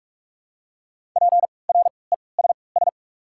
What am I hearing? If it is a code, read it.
Morse code, heard as a keyed tone, about 43 wpm, 700 Hz. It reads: P R E S S
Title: 3x Character Spacing - Top 300 Words - 45wpm